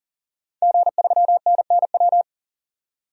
Morse code, heard as a keyed tone, about 40 wpm, 700 Hz. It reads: G3NNW
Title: DXCC-100 Call Signs - 5 Characters - 40wpm